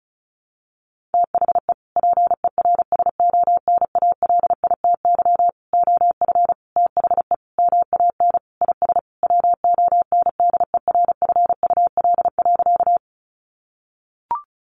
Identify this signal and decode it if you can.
Morse code, heard as a keyed tone, about 34 wpm, 700 Hz. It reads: THE PERSONALITY OF THE MAN IS WONDERFUL.
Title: Sentences from Top 1400 Words 35wpm